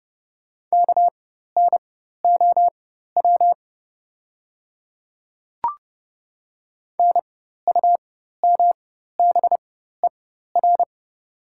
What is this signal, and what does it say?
Morse code, heard as a keyed tone, about 31 wpm, spaced as for 14 wpm, 700 Hz. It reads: KNOW NUMBER
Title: 4x Character Spacing - Top 100 Words - 30wpm